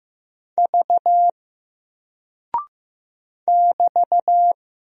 Morse code, heard as a keyed tone, about 15 wpm, 700 Hz. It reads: V <BT>
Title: Single Character (Letter:Number:Punctuation:Prosign) 15wpm